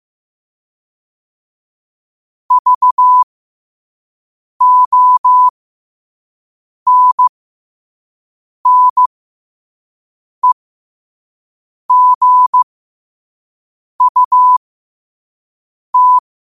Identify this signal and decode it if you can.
Morse code, heard as a keyed tone, about 15 wpm, spaced as for 5 wpm, 1000 Hz. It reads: VONNEGUT
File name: qotdfiles / 5wpm